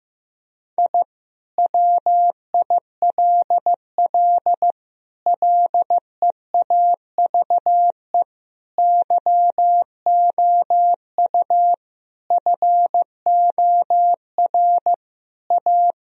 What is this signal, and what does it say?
Morse code, heard as a keyed tone, about 16 wpm, 700 Hz. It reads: I WILL LEAVE YOU FOR A
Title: Sentences from Top 1000 Words - Review Part 4 - 15wpm